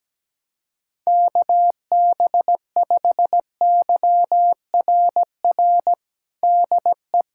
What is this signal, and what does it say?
Morse code, heard as a keyed tone, about 17 wpm, 700 Hz. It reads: KB5YRR DE
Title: USA Callsign DE Callsign 17wpm